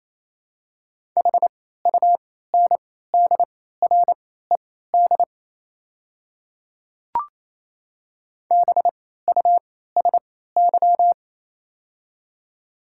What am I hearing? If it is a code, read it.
Morse code, heard as a keyed tone, about 28 wpm, spaced as for 16 wpm, 700 Hz. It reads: HUNDRED BUSY
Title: 3x Character Spacing - Top 500 Words - 28wpm